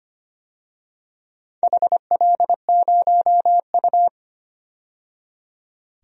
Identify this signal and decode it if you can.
Morse code, heard as a keyed tone, about 25 wpm, 700 Hz. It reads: HL0U
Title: DXCC-100 Call Signs - 4 Characters - 25wpm